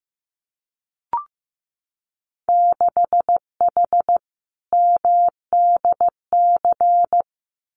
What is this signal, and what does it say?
Morse code, heard as a keyed tone, about 15 wpm, 700 Hz. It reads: 6H MDC
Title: Winter Field Day Exchanges Encore 15wpm